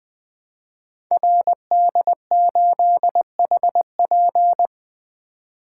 Morse code, heard as a keyed tone, about 20 wpm, 700 Hz. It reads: RD8HP
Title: Code Groups - 5 Letter:Numbers - 20wpm